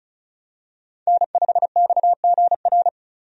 Morse code, heard as a keyed tone, about 34 wpm, 700 Hz. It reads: N5XGR
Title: USA Call Signs - 5 Characters - 35wpm